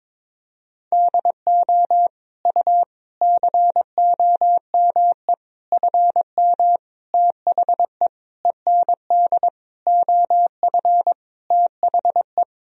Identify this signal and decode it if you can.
Morse code, heard as a keyed tone, about 22 wpm, 700 Hz. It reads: DO U COME FM THE END OF THE
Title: Sentences from Top 200 Words and Abbreviations 22wpm